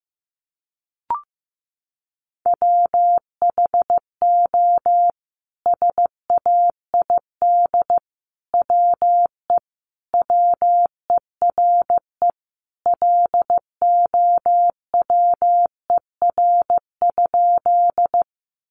Morse code, heard as a keyed tone, about 15 wpm, 700 Hz. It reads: WHO SAID WE WERE LOWER?
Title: Sentences from Top 700 Words 15wpm